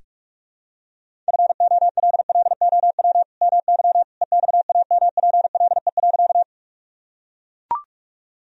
Morse code, heard as a keyed tone, about 44 wpm, 700 Hz. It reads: FOLLOW MY EXAMPLE.
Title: Sentences from Top 1000 Words - Review Part 1 - 45wpm